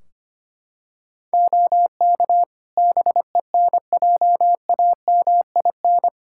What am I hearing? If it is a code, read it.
Morse code, heard as a keyed tone, about 25 wpm, 700 Hz. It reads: OK BENJAMIN
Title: K1USN SST QSO Practice 25wpm